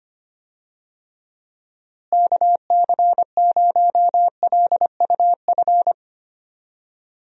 Morse code, heard as a keyed tone, about 25 wpm, 700 Hz. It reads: KC0LUF
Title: USA Call Signs - 6 Characters - Repeated 2x - 25wpm